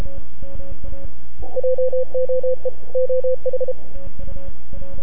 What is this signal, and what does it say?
Morse code, heard as a keyed tone, about 32 wpm, 510 Hz. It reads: JOE OH
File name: NAQPCW-8-2009 / W1UJ NAQP CW 09 0100Z